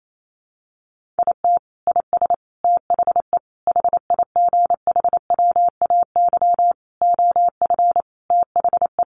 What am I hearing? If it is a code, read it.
Morse code, heard as a keyed tone, about 28 wpm, 700 Hz. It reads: IT IS THE HIGHWAY OF THE
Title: Sentences from Top 1400 Words 28wpm